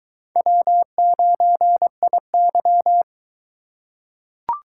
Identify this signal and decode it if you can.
Morse code, heard as a keyed tone, about 21 wpm, 700 Hz. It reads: W9IY